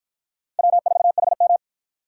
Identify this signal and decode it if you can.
Morse code, heard as a keyed tone, about 52 wpm, 700 Hz. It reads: W4HM